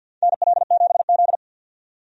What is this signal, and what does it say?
Morse code, heard as a keyed tone, about 52 wpm, 700 Hz. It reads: NL7Z
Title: USA Call Signs - 4 Characters - 50wpm